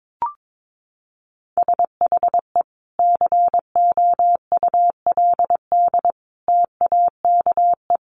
Morse code, heard as a keyed tone, about 22 wpm, 700 Hz. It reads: SHE COULD TAKE
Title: Sentences from Top 900 Words 22wpm